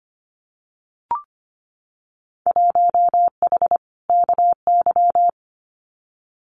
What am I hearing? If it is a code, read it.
Morse code, heard as a keyed tone, about 25 wpm, 700 Hz. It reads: 1H KY
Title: Winter Field Day Exchanges 25wpm